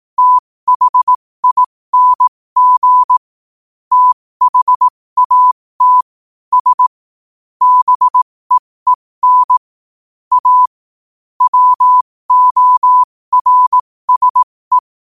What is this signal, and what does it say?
Morse code, heard as a keyed tone, about 18 wpm, spaced as for 15 wpm, 1000 Hz. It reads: THING THAT S BEEN A WORSE
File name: qotdfiles / 15wpm